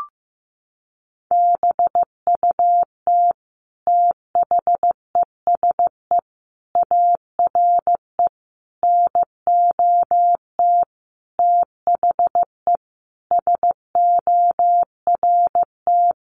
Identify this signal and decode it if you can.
Morse code, heard as a keyed tone, about 15 wpm, 700 Hz. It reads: BUT THESE ARE NOT THE SORT
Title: Sentences from Top 1300 Words 15wpm